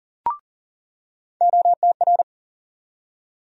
Morse code, heard as a keyed tone, about 40 wpm, 700 Hz. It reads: OTR
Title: USA Call Sign Postfixes - 3 Letters 40wpm